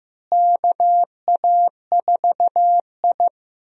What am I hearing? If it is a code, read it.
Morse code, heard as a keyed tone, about 16 wpm, 700 Hz. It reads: KA4I
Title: USA Call Signs - 4 Characters - Repeated 3x - 15wpm